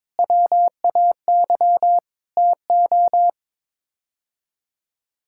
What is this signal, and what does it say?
Morse code, heard as a keyed tone, about 22 wpm, 700 Hz. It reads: WAY TO
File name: practice / 22wpm / Sets of 8 Words - 22wpm